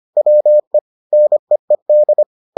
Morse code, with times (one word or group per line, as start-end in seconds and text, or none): 0.2-0.8 WE
1.1-2.2 NEED